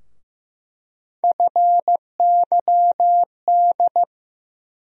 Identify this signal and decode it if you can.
Morse code, heard as a keyed tone, about 15 wpm, 700 Hz. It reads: FYD
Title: USA Call Sign Postfixes - 3 Letters 15wpm